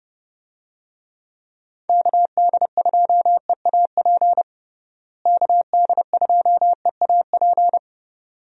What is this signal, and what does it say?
Morse code, heard as a keyed tone, about 30 wpm, 700 Hz. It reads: KD2EAP KD2EAP
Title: USA Call Signs - 6 Characters - Repeated 2x - 30wpm